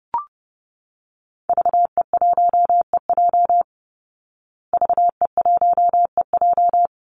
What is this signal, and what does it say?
Morse code, heard as a keyed tone, about 30 wpm, 700 Hz. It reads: VE1EJ VE1EJ
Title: Call Signs Repeated 3x - 30wpm